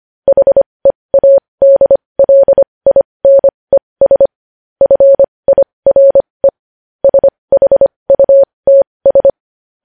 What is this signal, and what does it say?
Morse code, heard as a keyed tone, about 25 wpm, spaced as for 20 wpm, 550 Hz. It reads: HEADLINES FIRE SHUTS
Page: news-headlines-in-morse-code-at-20-wpm